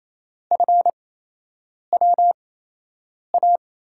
Morse code, heard as a keyed tone, about 28 wpm, 700 Hz. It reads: F W A
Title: USA Call Signs - 6 Characters - 8x Character Spacing - 28wpm